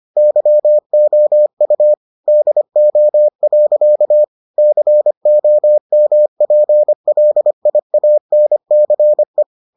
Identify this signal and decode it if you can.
Morse code, heard as a keyed tone, about 25 wpm, 600 Hz. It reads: YOU DO. COMPLIANCE